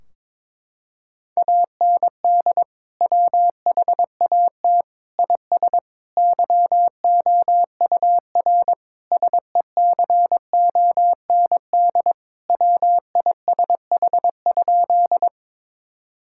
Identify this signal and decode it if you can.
Morse code, heard as a keyed tone, about 22 wpm, 700 Hz. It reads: AND WHAT IS YOUR SECOND WISH?
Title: Sentences from Top 1600 Words 22wpm